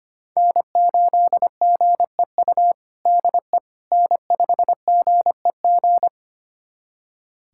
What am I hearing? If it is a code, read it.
Morse code, heard as a keyed tone, about 25 wpm, 700 Hz. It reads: N8GEU DE N5GEG
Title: USA Callsign DE Callsign - 5 Characters - 25wpm